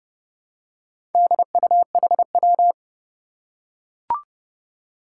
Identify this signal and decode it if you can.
Morse code, heard as a keyed tone, about 30 wpm, 700 Hz. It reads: DUHW
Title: Four Challenging Letter-Number Combinations 30wpm